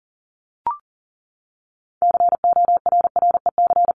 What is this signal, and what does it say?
Morse code, heard as a keyed tone, about 40 wpm, 700 Hz. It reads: CORREC
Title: Top 500 Words 40wpm